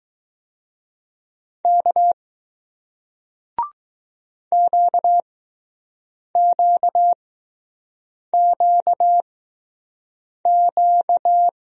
Morse code, heard as a keyed tone, about 20 wpm, spaced as for 6 wpm, 700 Hz. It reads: K QQQQ